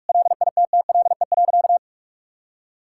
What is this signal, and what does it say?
Morse code, heard as a keyed tone, about 45 wpm, 700 Hz. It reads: LITTLE.
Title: Sentences from Top 1600 Words 45wpm